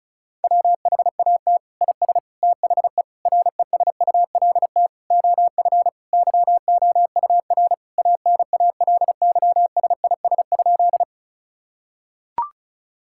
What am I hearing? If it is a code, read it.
Morse code, heard as a keyed tone, about 36 wpm, 700 Hz. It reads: WHAT IS THE RESULT OF YOUR ANALYSIS?
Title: Sentences from Top 1000 Words - Review Part 5 - 35wpm